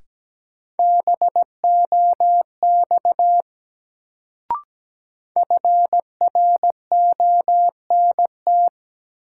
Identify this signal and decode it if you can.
Morse code, heard as a keyed tone, about 17 wpm, 700 Hz. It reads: BOX FRONT
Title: Sentences from Top 500 Words - New Word Drill - 17wpm